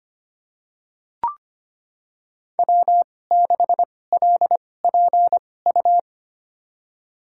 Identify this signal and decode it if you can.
Morse code, heard as a keyed tone, about 25 wpm, spaced as for 18 wpm, 700 Hz. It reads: W6LPU
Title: USA Call Signs - 5 Characters - 2x Character Spacing - 25wpm